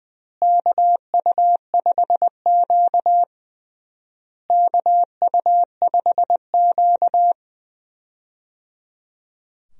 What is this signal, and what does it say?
Morse code, heard as a keyed tone, about 20 wpm, 700 Hz. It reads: KU5Q KU5Q
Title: USA Call Signs - 4 Characters - Repeated 3x - 20wpm